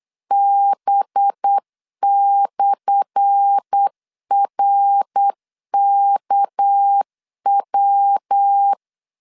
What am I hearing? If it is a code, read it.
Morse code, heard as a keyed tone, about 8 wpm, 800 Hz. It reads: B/RKW
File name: LEZIONI / 10 SIGNS / 10A SLASH 8wpm 3452